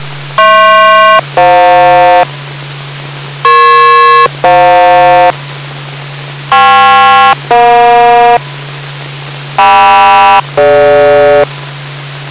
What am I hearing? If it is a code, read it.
Aircraft SELCAL, heard as two pairs of tones. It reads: HP-FJ EP-FJ KP-EJ KM-DG